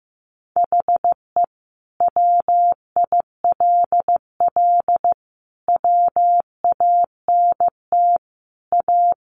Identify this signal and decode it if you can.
Morse code, heard as a keyed tone, about 15 wpm, 700 Hz. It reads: HE WILL WANT A
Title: Sentences from Top 1300 Words 15wpm